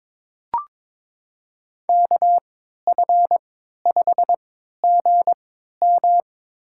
Morse code, heard as a keyed tone, about 22 wpm, spaced as for 12 wpm, 700 Hz. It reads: KF5GM
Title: USA Call Signs - 3x Character Spacing - 22wpm